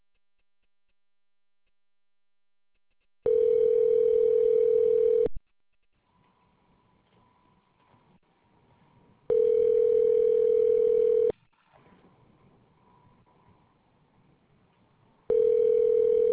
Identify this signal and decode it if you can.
Ringback tone (North America), 440 + 480 Hz.